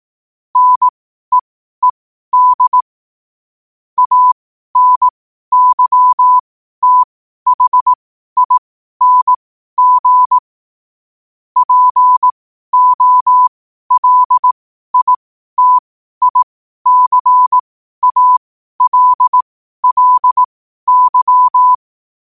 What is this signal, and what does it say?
Morse code, heard as a keyed tone, about 18 wpm, spaced as for 12 wpm, 1000 Hz. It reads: NEED ANYTHING POLITICALLY
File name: qotdfiles / 12wpm